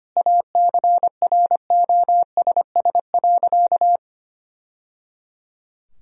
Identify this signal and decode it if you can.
Morse code, heard as a keyed tone, about 25 wpm, 700 Hz. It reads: ACROSS.
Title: Sentences from Top 1000 Words - Review Part 3 - 25wpm